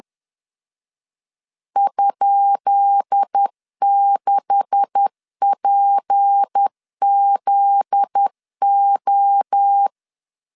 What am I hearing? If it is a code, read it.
Morse code, heard as a keyed tone, about 10 wpm, 800 Hz. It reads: ?6PZO
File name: LEZIONI / 10 SIGNS / 10D COMA 10wpm 3013